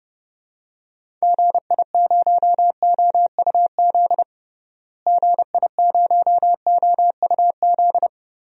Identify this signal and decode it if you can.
Morse code, heard as a keyed tone, about 32 wpm, 700 Hz. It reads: GI0OUZ GI0OUZ
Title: DXCC-100 Call Signs - 6 Characters - Repeated 3x - 30wpm